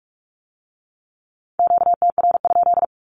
Morse code, heard as a keyed tone, about 46 wpm, 700 Hz. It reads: QTR?
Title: Q-Codes with Spoken Meaning - 45wpm